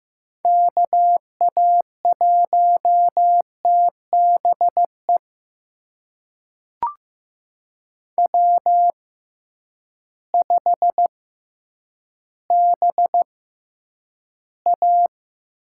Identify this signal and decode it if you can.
Morse code, heard as a keyed tone, about 15 wpm, spaced as for 9 wpm, 700 Hz. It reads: KA1TBE W5BA
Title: USA Call Signs - 6x Character Spacing - 15wpm